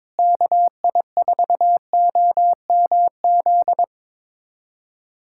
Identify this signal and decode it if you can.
Morse code, heard as a keyed tone, about 22 wpm, 700 Hz. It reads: KI4OMZ
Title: USA Callsign DE Callsign 22wpm